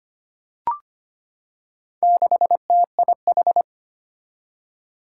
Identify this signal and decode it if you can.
Morse code, heard as a keyed tone, about 25 wpm, 700 Hz. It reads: 6TIH
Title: Four Challenging Letter-Number Combinations 25wpm